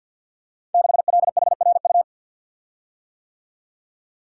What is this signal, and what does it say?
Morse code, heard as a keyed tone, about 52 wpm, 700 Hz. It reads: 6LHRU
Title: Code Groups - 5 Letter:Numbers - 50wpm